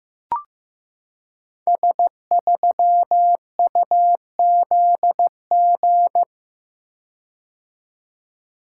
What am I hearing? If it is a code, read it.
Morse code, heard as a keyed tone, about 15 wpm, 700 Hz. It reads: S3UZG
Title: Code Groups - 5 Letter:Numbers - 15wpm